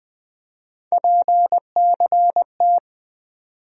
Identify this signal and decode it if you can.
Morse code, heard as a keyed tone, about 20 wpm, 700 Hz. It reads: PCT
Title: USA Call Sign Postfixes - 3 Letters 20wpm